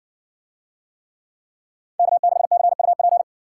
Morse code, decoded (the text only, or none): K6CUP